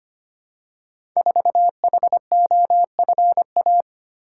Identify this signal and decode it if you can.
Morse code, heard as a keyed tone, about 25 wpm, 700 Hz. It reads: 4HOFA
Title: Code Groups - 5 Letter:Numbers - 25wpm